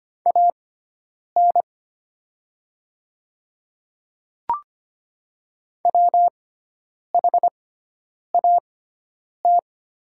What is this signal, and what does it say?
Morse code, heard as a keyed tone, about 25 wpm, spaced as for 9 wpm, 700 Hz. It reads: AN WHAT
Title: 6x Character Spacing - Top 100 Words - 25wpm